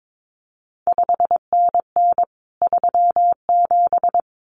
Morse code, heard as a keyed tone, about 22 wpm, 700 Hz. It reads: 5NN 37
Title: CQ WW DX Contest Exchange Encore - 22wpm